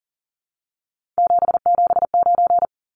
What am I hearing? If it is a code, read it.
Morse code, heard as a keyed tone, about 40 wpm, 700 Hz. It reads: 779
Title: Numbers - 3-Digit 40wpm